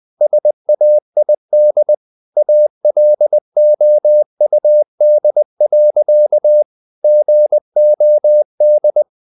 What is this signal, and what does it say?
Morse code, heard as a keyed tone, about 20 wpm, 600 Hz. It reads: SAID ALOUD. GOD
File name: the old man and the sea / the old man and the sea part 1 20wpm